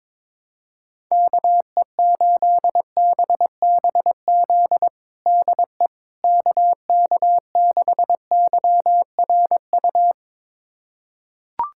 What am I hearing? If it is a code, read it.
Morse code, heard as a keyed tone, about 22 wpm, 700 Hz. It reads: KE8BBZ DE KK6YRU
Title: USA Callsign DE Callsign - 6 Characters - 22wpm